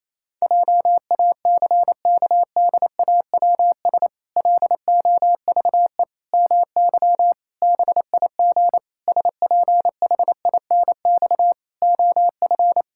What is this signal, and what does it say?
Morse code, heard as a keyed tone, about 28 wpm, 700 Hz. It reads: JACKDAWS LOVE MY BIG SPHINX OF
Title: Pangram Challenge v2 - 28wpm